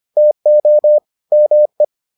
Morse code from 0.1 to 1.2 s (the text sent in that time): TO